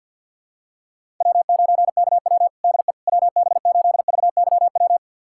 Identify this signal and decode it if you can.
Morse code, heard as a keyed tone, about 45 wpm, 700 Hz. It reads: W9KW DE WB8VYW
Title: USA Callsign DE Callsign 50wpm